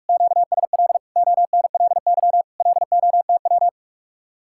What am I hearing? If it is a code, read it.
Morse code, heard as a keyed tone, about 49 wpm, 700 Hz. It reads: QSL ONLY LOTW